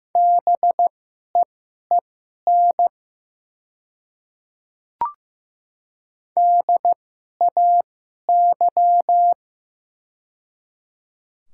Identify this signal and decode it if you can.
Morse code, heard as a keyed tone, about 15 wpm, spaced as for 11 wpm, 690 Hz. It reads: BEEN DAY
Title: 2x Character Spacing - Top 100 Words - 15wpm